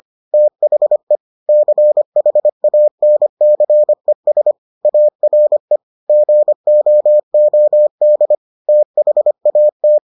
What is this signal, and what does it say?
Morse code, heard as a keyed tone, about 25 wpm, 600 Hz. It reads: THE CHANCES ARE GOOD THAT